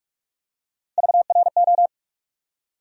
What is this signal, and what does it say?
Morse code, heard as a keyed tone, about 46 wpm, 700 Hz. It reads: VRO
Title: USA Call Sign Postfixes - 3 Letters 45wpm